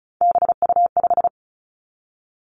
Morse code, heard as a keyed tone, about 36 wpm, 700 Hz. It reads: BU5
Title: Three Challenging Letter-Number Combinations 35wpm